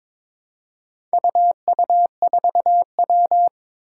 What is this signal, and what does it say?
Morse code, heard as a keyed tone, about 22 wpm, 700 Hz. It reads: UU4W
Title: Four Challenging Letter-Number Combinations 22wpm